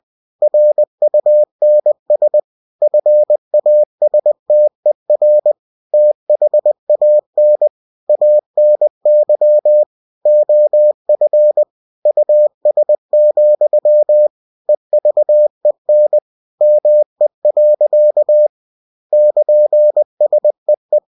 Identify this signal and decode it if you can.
Morse code, heard as a keyed tone, about 20 wpm, 600 Hz. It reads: RUNS FASTER THAN ANY OF US, EVEN ME. <KN>SEE
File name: cw-books / winnie-the-pooh / winnie-20wpm-page-0063